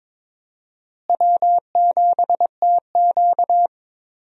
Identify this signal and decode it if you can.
Morse code, heard as a keyed tone, about 21 wpm, 700 Hz. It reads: W7TQ
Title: USA Call Signs - 4 Characters - Repeated 3x - 22wpm